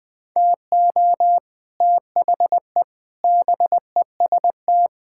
Morse code, heard as a keyed tone, about 20 wpm, 700 Hz. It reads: TO THE BEST
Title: Sets of 8 Words - 20wpm